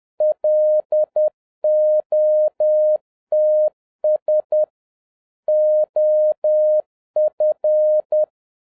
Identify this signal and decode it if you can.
Morse code, heard as a keyed tone, about 10 wpm, 610 Hz. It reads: LOTS OF